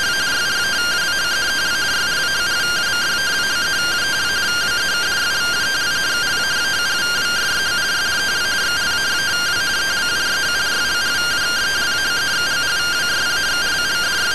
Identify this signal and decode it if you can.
Two-tone teleprinter signal, about 170 Hz shift.